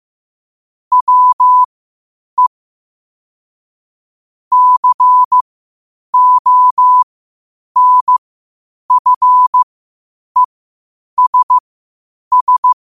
Morse code, heard as a keyed tone, about 15 wpm, spaced as for 8 wpm, 1000 Hz. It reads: WE CONFESS